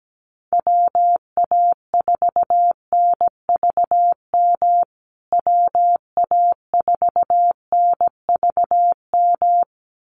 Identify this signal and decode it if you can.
Morse code, heard as a keyed tone, about 17 wpm, 700 Hz. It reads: WA4NVM WA4NVM
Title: USA Call Signs Repeated 2x 17wpm